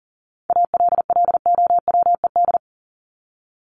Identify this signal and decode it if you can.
Morse code, heard as a keyed tone, about 40 wpm, 700 Hz. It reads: ALLOWED
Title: ED Suffix - Top 100 Words with Base Words 40wpm